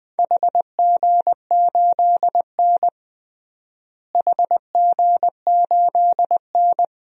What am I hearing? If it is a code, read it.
Morse code, heard as a keyed tone, about 20 wpm, 700 Hz. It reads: HG8N HG8N
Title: DXCC-100 Call Signs - 4 Characters - Repeated 2x - 20wpm